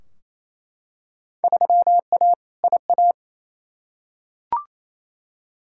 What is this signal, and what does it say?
Morse code, heard as a keyed tone, about 28 wpm, 700 Hz. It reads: 3A IA
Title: Field Day Exchanges Encore - 28wpm